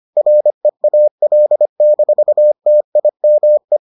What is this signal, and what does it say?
Morse code, heard as a keyed tone, about 25 wpm, 600 Hz. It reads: REAL-TIME